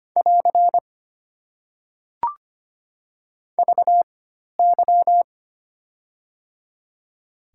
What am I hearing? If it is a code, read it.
Morse code, heard as a keyed tone, about 25 wpm, spaced as for 12 wpm, 690 Hz. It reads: <AR> VY
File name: Top 90 QSO Elements - 4x Character Spacing - 25wpm